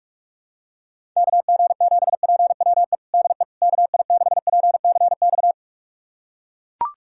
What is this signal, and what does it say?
Morse code, heard as a keyed tone, about 41 wpm, 700 Hz. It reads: KG7PWE DE KI6PCX
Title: USA Callsign DE Callsign - 6 Characters - 45wpm